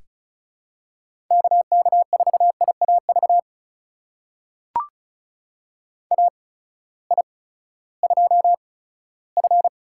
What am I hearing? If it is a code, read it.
Morse code, heard as a keyed tone, about 36 wpm, spaced as for 18 wpm, 700 Hz. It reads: KK4IAV AI2F
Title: USA Call Signs - 8x Character Spacing - 35wpm